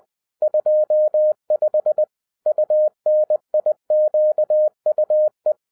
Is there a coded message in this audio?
Yes: Morse code from 0.4 to 5.5 s, heard as a keyed tone, about 20 wpm, 610 Hz. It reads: 25 UNIQUE